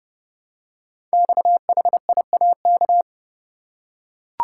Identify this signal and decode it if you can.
Morse code, heard as a keyed tone, about 30 wpm, 700 Hz. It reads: XHIAK
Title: Code Groups - 5 Letter:Numbers - 30wpm